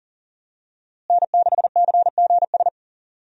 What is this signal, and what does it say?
Morse code, heard as a keyed tone, about 40 wpm, 700 Hz. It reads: N6CGS
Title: USA Call Signs - 5 Characters - Repeated 3x - 40wpm